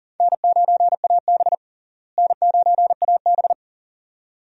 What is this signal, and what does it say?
Morse code, heard as a keyed tone, about 40 wpm, 700 Hz. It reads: N9AB N9AB
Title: USA Call Signs - 4 Characters - Repeated 2x - 40wpm